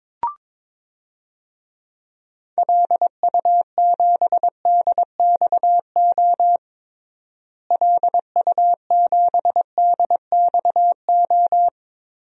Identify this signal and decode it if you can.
Morse code, heard as a keyed tone, about 22 wpm, 700 Hz. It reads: LU7DXO LU7DXO
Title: DXCC-100 Call Signs - 6 Characters - Repeated 2x - 22wpm